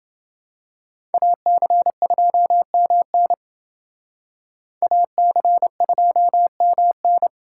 Morse code, heard as a keyed tone, about 27 wpm, 700 Hz. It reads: AC2MN AC2MN